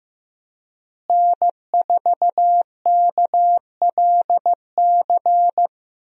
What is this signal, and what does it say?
Morse code, heard as a keyed tone, about 15 wpm, 700 Hz. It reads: N4KLC
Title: USA Call Signs - 5 Characters - Repeated 2x - 15wpm